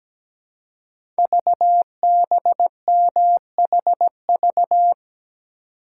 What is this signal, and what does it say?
Morse code, heard as a keyed tone, about 17 wpm, 700 Hz. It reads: VBMHV
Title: Code Groups - 5 Letters - 17wpm